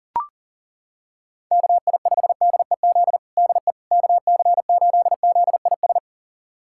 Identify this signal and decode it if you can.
Morse code, heard as a keyed tone, about 37 wpm, 700 Hz. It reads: KI5DEZ DE KC8ZIS
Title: USA Callsign DE Callsign 40wpm